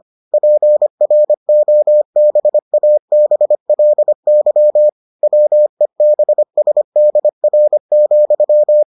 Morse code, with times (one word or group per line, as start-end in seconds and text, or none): 0.3-4.9 PROBABLY
5.2-8.8 WEBSDR,